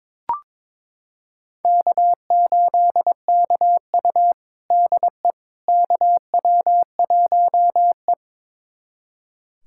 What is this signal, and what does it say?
Morse code, heard as a keyed tone, about 22 wpm, 700 Hz. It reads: K8KU DE KW1E
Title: USA Callsign DE Callsign - 4 Characters - 22wpm